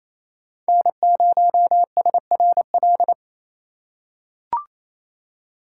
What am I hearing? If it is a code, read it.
Morse code, heard as a keyed tone, about 28 wpm, 700 Hz. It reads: N0SRL
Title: USA Call Signs - 5 Characters - 2x Character Spacing - 28wpm